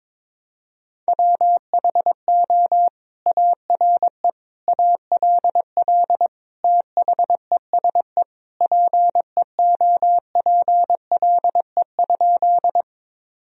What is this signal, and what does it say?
Morse code, heard as a keyed tone, about 22 wpm, 700 Hz. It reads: WHO ARE ALL THESE PEOPLE?
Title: Sentences from Top 1000 Words - Review Part 1 - 22wpm